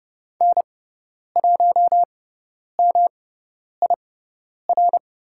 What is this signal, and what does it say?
Morse code, heard as a keyed tone, about 30 wpm, spaced as for 10 wpm, 700 Hz. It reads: N1MIR
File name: Lesson 18g - Letter m - Call Signs 30wpm